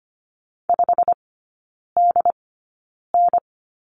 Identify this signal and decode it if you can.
Morse code, heard as a keyed tone, about 25 wpm, spaced as for 9 wpm, 700 Hz. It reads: 5DN
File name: Lesson 15e - Number 5 - Three Characters 25wpm